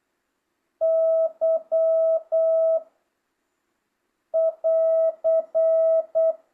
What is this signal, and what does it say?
Morse code, heard as a keyed tone, about 8 wpm, 650 Hz. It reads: Y <AR>